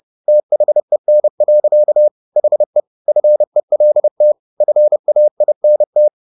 Morse code, heard as a keyed tone, about 30 wpm, 600 Hz. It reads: THEN. HE FELT FAINT